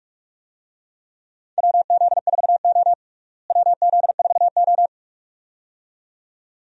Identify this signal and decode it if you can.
Morse code, heard as a keyed tone, about 45 wpm, 700 Hz. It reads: WZ4O WZ4O
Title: USA Call Signs - 4 Characters - Repeated 2x - 45wpm